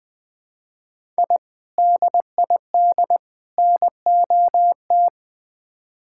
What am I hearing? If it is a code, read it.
Morse code, heard as a keyed tone, about 20 wpm, 700 Hz. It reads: I DID NOT
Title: Build A Phrase - 4-Word Phrases - 20wpm